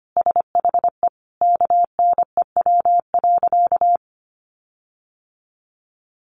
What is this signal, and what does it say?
Morse code, heard as a keyed tone, about 25 wpm, 700 Hz. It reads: SHE KNEW.
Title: Sentences from Top 1300 Words 25wpm